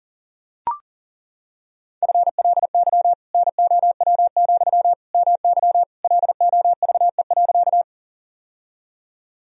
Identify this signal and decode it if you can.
Morse code, heard as a keyed tone, about 40 wpm, 700 Hz. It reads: FLY NOW, MY LOVE.
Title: Sentences from Top 1000 Words - Review Part 2 - 40wpm